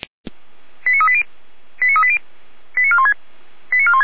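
Five-tone selective call (ZVEI), heard as quick runs of five tones, tones of about 70 ms. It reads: 89380 89380 89416 89416
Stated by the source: ZVEI five-tone selective call 89108 received via SDR, archived by Feuerwehr Orlamünde (fire brigade).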